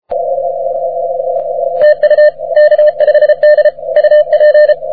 Morse code, heard as a keyed tone, about 32 wpm, 600 Hz. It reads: TU K5D UP
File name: wp / k5d 20m cw